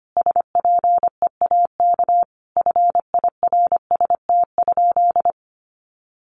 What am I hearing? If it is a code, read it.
Morse code, heard as a keyed tone, about 25 wpm, 700 Hz. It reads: SPEAK FIRST?
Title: Sentences from Top 1100 Words 25wpm